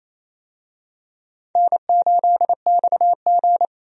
Morse code, heard as a keyed tone, about 28 wpm, 700 Hz. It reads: N8XG
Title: USA Call Signs - 4 Characters - 28wpm